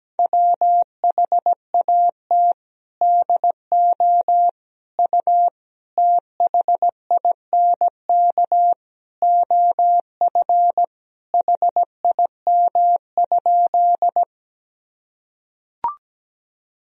Morse code, heard as a keyed tone, about 17 wpm, 700 Hz. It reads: WHAT DO U THINK OF HIM?
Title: Sentences from Top 200 Words and Abbreviations 17wpm